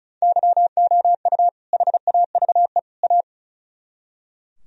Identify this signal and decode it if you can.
Morse code, heard as a keyed tone, about 47 wpm, spaced as for 41 wpm, 700 Hz. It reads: YOU HAVE A
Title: Speed-Racing - Sets of 3 Words 35wpm